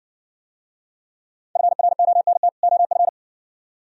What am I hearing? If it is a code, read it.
Morse code, heard as a keyed tone, about 58 wpm, 700 Hz. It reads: FRONT OF